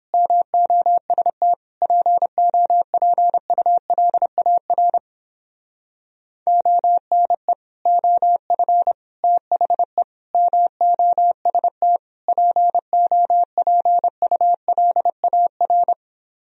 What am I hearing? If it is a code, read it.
Morse code, heard as a keyed tone, about 31 wpm, 700 Hz. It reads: MOST POPULAR ONE OF THE MOST POPULAR